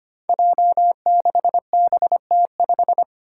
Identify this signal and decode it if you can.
Morse code, heard as a keyed tone, about 25 wpm, 700 Hz. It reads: J6BT5
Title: Five Challenging Letter-Number Combinations 25wpm